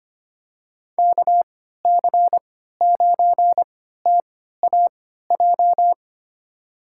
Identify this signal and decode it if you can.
Morse code, heard as a keyed tone, about 25 wpm, spaced as for 14 wpm, 700 Hz. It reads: KC9TAJ
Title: USA Call Signs - 6 Characters - 3x Character Spacing - 25wpm